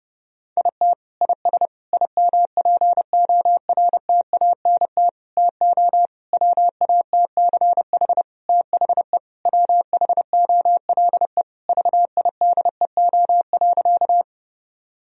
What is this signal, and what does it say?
Morse code, heard as a keyed tone, about 28 wpm, 700 Hz. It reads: IT IS IMPORTANT TO WATCH THE WHOLE VIDEO.
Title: Sentences from Top 1000 Words - Review Part 4 - 30wpm